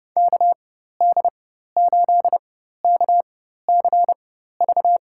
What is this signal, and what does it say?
Morse code, heard as a keyed tone, about 30 wpm, spaced as for 14 wpm, 700 Hz. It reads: KD8KCV
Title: USA Call Signs - 4x Character Spacing - 30wpm